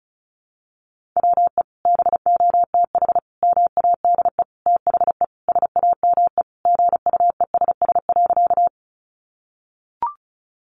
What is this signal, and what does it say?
Morse code, heard as a keyed tone, about 34 wpm, 700 Hz. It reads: WE BOTH MADE THE SAME GUESS.
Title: Sentences from Top 1500 Words 35wpm